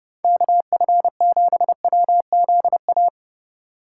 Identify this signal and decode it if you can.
Morse code, heard as a keyed tone, about 30 wpm, 700 Hz. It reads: KF7WZA
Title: USA Call Signs - 6 Characters - 4x Character Spacing - 30wpm